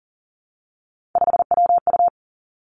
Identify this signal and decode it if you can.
Morse code, heard as a keyed tone, about 40 wpm, 700 Hz. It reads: 5WU
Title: Three Challenging Letter-Number Combinations 40wpm